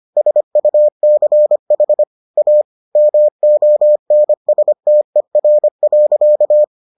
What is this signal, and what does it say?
Morse code, heard as a keyed tone, about 25 wpm, 600 Hz. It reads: SUCH A MONSTER.